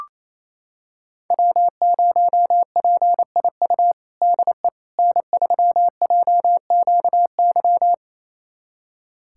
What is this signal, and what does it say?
Morse code, heard as a keyed tone, about 28 wpm, 700 Hz. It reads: W0PIU DE N3JQY
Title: USA Callsign DE Callsign - 5 Characters - 28wpm